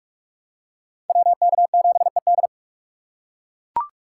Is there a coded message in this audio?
Yes: Morse code from 1.1 to 2.5 s, heard as a keyed tone, about 41 wpm, 700 Hz. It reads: WK7ED